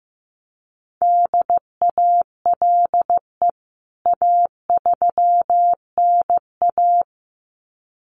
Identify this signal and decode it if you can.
Morse code, heard as a keyed tone, about 15 wpm, 700 Hz. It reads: DALE A3NA
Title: CWT Exchange 15wpm